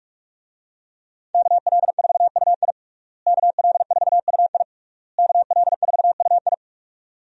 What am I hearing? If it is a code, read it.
Morse code, heard as a keyed tone, about 42 wpm, 700 Hz. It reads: KL4UI KL4UI KL4UI
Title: USA Call Signs - 5 Characters - Repeated 3x - 45wpm